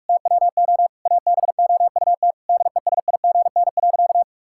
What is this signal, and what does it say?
Morse code, heard as a keyed tone, about 46 wpm, 700 Hz. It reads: TWO ABOUT DESIGN.
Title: Sentences from Top 1000 Words - Review Part 5 - 45wpm